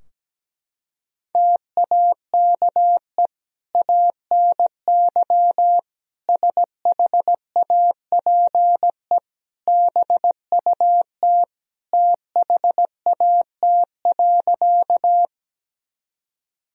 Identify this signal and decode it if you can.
Morse code, heard as a keyed tone, about 17 wpm, 700 Hz. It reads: TAKE ANY SHAPE BUT THAT.
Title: Sentences from Top 1000 Words - Review Part 3 - 17wpm